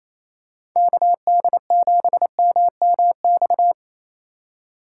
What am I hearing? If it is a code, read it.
Morse code, heard as a keyed tone, about 28 wpm, 700 Hz. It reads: KD7MMX
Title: USA Call Signs - 6 Characters - 3x Character Spacing - 28wpm